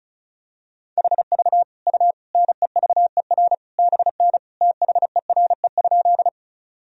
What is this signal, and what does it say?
Morse code, heard as a keyed tone, about 34 wpm, 700 Hz. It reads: HV U NEVER BN THERE?
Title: Sentences from Top 300 Words and Abbreviations 35wpm